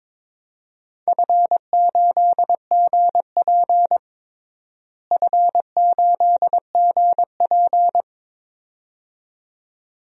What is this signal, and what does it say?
Morse code, heard as a keyed tone, about 22 wpm, 700 Hz. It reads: F8GP F8GP
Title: DXCC-100 Call Signs - 4 Characters - Repeated 2x - 22wpm